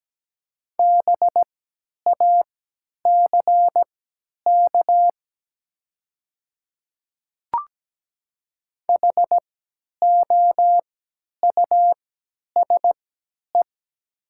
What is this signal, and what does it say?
Morse code, heard as a keyed tone, about 17 wpm, spaced as for 10 wpm, 700 Hz. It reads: BACK HOUSE
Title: 3x Character Spacing - Top 200 Words - 17wpm